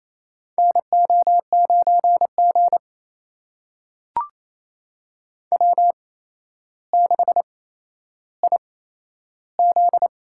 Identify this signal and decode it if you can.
Morse code, heard as a keyed tone, about 28 wpm, spaced as for 12 wpm, 700 Hz. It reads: NO9G W6IZ
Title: USA Call Signs - 8x Character Spacing - 28wpm